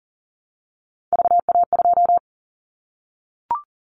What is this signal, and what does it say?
Morse code, heard as a keyed tone, about 40 wpm, 700 Hz. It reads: VA2
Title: Callsign Prefixes for Top 100 DXCC - 40wpm